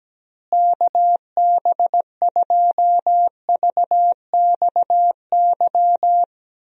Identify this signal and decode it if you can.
Morse code, heard as a keyed tone, about 17 wpm, 700 Hz. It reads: KB2VXY
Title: USA Call Signs - 6 Characters - Repeated 3x - 17wpm